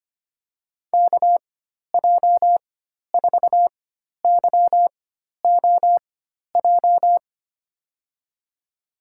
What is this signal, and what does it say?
Morse code, heard as a keyed tone, about 25 wpm, spaced as for 12 wpm, 700 Hz. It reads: KJ4YOJ
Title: USA Call Signs - 6 Characters - 4x Character Spacing - 25wpm